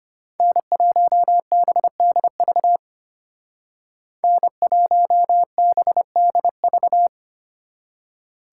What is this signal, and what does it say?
Morse code, heard as a keyed tone, about 30 wpm, 700 Hz. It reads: N1BDV N1BDV
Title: Speed Racing - USA 1x3 Call Signs 25wpm